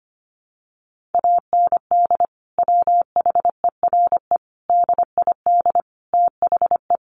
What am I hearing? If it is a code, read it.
Morse code, heard as a keyed tone, about 25 wpm, 700 Hz. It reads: AND WHERE DID THE
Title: Sentences from Top 1500 Words 25wpm